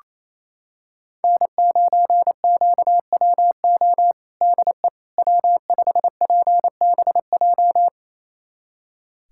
Morse code, heard as a keyed tone, about 27 wpm, 700 Hz. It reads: N9QWO DE W5PBJ
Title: USA Callsign DE Callsign - 5 Characters - 28wpm